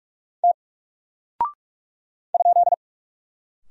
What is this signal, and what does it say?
Morse code, heard as a keyed tone, about 44 wpm, 700 Hz. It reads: T ?
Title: Single Character (Letter:Number:Punctuation:Prosign) Alt 45wpm